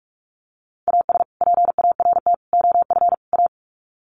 Morse code, heard as a keyed tone, about 44 wpm, 700 Hz. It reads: AS PART OF A
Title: Sets of 4 Words 45wpm